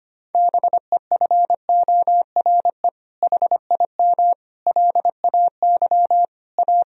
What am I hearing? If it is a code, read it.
Morse code, heard as a keyed tone, about 25 wpm, 700 Hz. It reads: BEFORE HIM LAY A
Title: Sentences from Top 1000 Words - Review Part 2 - 25wpm